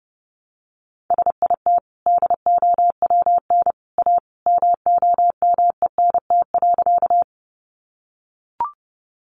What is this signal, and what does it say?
Morse code, heard as a keyed tone, about 30 wpm, 700 Hz. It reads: SIT DOWN A MOMENT.
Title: Sentences from Top 1000 Words 30wpm